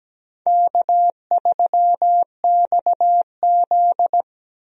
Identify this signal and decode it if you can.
Morse code, heard as a keyed tone, about 17 wpm, 700 Hz. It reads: K3XZ
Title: USA Call Signs - 4 Characters - 17wpm